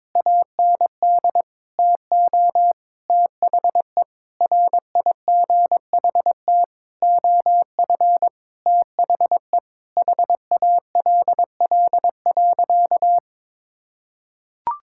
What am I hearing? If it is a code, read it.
Morse code, heard as a keyed tone, about 22 wpm, 700 Hz. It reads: AND TO THE RIGHT OF THE HALL.
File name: practice / 22wpm / Sentences with 8-Word Phrases - 22wpm